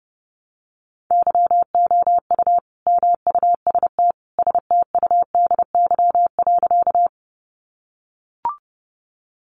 Morse code, heard as a keyed tone, about 30 wpm, 700 Hz. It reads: YOU MUST STUDY.
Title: Sentences from Top 300 Words 30wpm